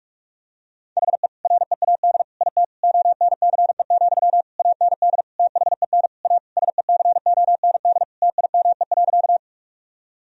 Morse code, heard as a keyed tone, about 46 wpm, 700 Hz. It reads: HE READ IT ONCE, AND THEN A SECOND TIME.
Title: Sentences from Top 1000 Words - Review Part 1 - 45wpm